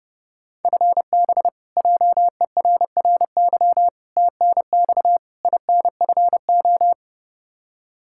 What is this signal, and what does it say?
Morse code, heard as a keyed tone, about 32 wpm, 700 Hz. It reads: FB JERRY TNX INFO
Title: K1USN SST QSO Practice 30wpm